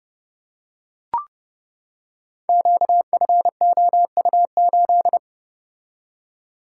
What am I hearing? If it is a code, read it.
Morse code, heard as a keyed tone, about 30 wpm, 700 Hz. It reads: QFOU8
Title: Code Groups - 5 Letter:Numbers - 30wpm